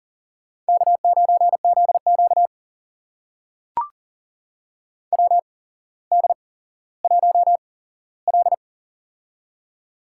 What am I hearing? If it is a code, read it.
Morse code, heard as a keyed tone, about 39 wpm, spaced as for 17 wpm, 700 Hz. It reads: K9ZQ WD1L
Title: USA Call Signs - 4 Characters - 8x Character Spacing - 40wpm